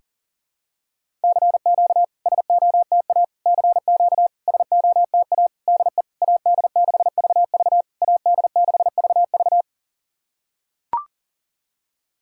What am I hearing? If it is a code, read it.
Morse code, heard as a keyed tone, about 40 wpm, 690 Hz. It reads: CQ SOTA CQ SOTA DE AD6VV AD6VV
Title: SOTA QSO Practice 40wpm